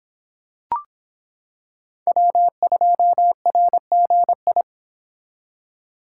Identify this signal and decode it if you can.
Morse code, heard as a keyed tone, about 26 wpm, 700 Hz. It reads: W2RGI